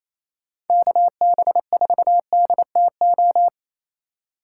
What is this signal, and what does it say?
Morse code, heard as a keyed tone, about 33 wpm, 700 Hz. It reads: KB4DTO
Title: Speed Racing - USA 2x3 Call Signs 28wpm